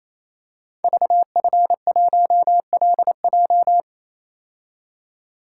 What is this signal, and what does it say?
Morse code, heard as a keyed tone, about 28 wpm, 700 Hz. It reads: VF1LJ
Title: Code Groups - 5 Letter:Numbers - 28wpm